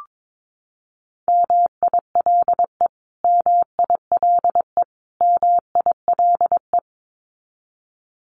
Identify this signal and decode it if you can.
Morse code, heard as a keyed tone, about 22 wpm, 700 Hz. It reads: MILE MILE MILE
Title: Top 500 Words Repeated 22wpm